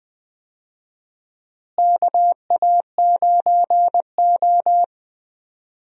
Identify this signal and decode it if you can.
Morse code, heard as a keyed tone, about 20 wpm, 700 Hz. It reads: KA9O
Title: USA Call Signs - 4 Characters - Repeated 2x - 20wpm